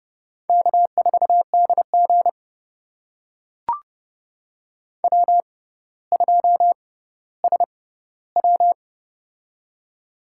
Morse code, heard as a keyed tone, about 32 wpm, spaced as for 16 wpm, 700 Hz. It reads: K4DG W2SW
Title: USA Call Signs - 6x Character Spacing - 30wpm